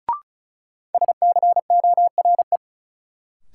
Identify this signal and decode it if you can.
Morse code, heard as a keyed tone, about 34 wpm, 700 Hz. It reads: SCORE
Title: Sentences from Top 1600 Words - New Word Drill - 35wpm